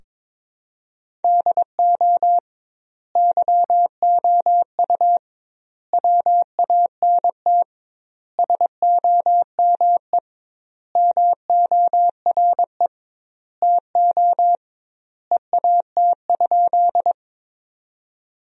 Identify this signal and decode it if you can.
Morse code, heard as a keyed tone, about 22 wpm, 700 Hz. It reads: DO YOU WANT SOME MORE TO EAT?
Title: 2x Word-Spacing - Sentences from Top 300 Words - 22wpm